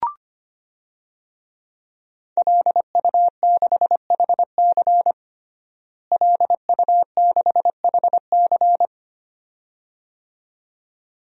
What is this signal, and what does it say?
Morse code, heard as a keyed tone, about 25 wpm, 700 Hz. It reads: LU6HC LU6HC
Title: DXCC-100 Call Signs - 5 Characters - Repeated 2x - 25wpm